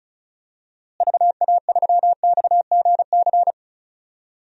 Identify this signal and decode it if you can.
Morse code, heard as a keyed tone, about 34 wpm, 700 Hz. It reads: VA3XGC
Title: DXCC-100 Call Signs - 6 Characters - Repeated 3x - 35wpm